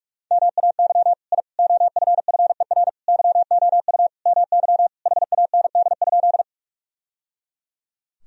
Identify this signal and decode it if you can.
Morse code, heard as a keyed tone, about 41 wpm, 700 Hz. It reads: MAY I OFFER YOU MY HAND?
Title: Sentences from Top 1000 Words - Review Part 4 - 45wpm